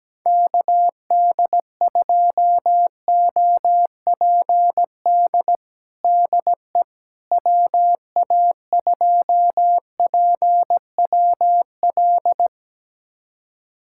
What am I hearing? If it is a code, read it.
Morse code, heard as a keyed tone, about 17 wpm, 700 Hz. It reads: KD2OPD DE WA2PWL
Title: USA Callsign DE Callsign 17wpm